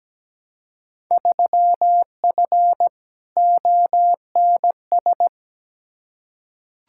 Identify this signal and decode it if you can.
Morse code, heard as a keyed tone, about 17 wpm, 700 Hz. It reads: 3F ONS
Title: Field Day Exchanges - 17wpm